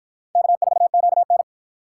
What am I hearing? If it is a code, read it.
Morse code, heard as a keyed tone, about 52 wpm, 700 Hz. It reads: K4QN